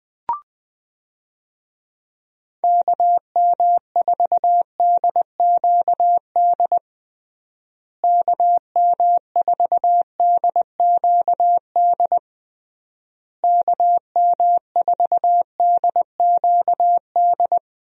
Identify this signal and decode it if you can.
Morse code, heard as a keyed tone, about 20 wpm, 700 Hz. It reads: KM4DQD KM4DQD KM4DQD
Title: USA Call Signs - 6 Characters - Repeated 3x - 20wpm